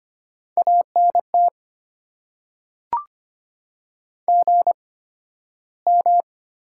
Morse code, heard as a keyed tone, about 25 wpm, spaced as for 13 wpm, 700 Hz. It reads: ANT GM
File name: Top 90 QSO Elements - 8x Character Spacing - 25wpm